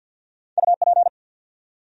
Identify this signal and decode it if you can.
Morse code, heard as a keyed tone, about 54 wpm, 700 Hz. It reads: UP
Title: Top 100 Words 50wpm